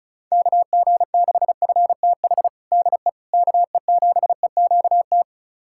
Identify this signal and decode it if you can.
Morse code, heard as a keyed tone, about 36 wpm, 700 Hz. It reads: KG6FTH DE KE7EQT
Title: USA Callsign DE Callsign - 6 Characters - 35wpm